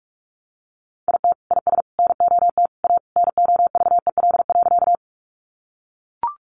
Morse code, heard as a keyed tone, about 44 wpm, 700 Hz. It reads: IT IS NOT A NOVEL.
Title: Sentences from Top 1300 Words 45wpm